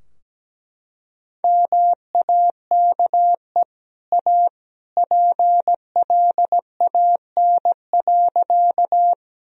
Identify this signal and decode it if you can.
Morse code, heard as a keyed tone, about 17 wpm, 700 Hz. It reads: MAKE A PLAN.
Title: Sentences from Top 1000 Words - Review Part 2 - 17wpm